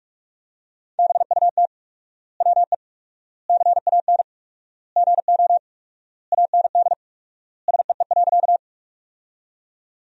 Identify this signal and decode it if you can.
Morse code, heard as a keyed tone, about 46 wpm, 700 Hz. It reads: BUT WE CAN GO AND SEE.
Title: 4x Word-Spacing - Sentences from Top 100 Words - 45wpm